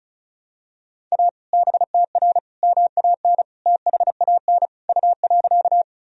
Morse code, heard as a keyed tone, about 34 wpm, 700 Hz. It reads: A BTR MAN THAN U.
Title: Sentences from Top 400 Words and Abbreviations 35wpm